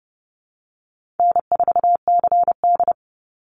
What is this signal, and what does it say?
Morse code, heard as a keyed tone, about 30 wpm, 700 Hz. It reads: N4CD
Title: Easy Call Signs - 30wpm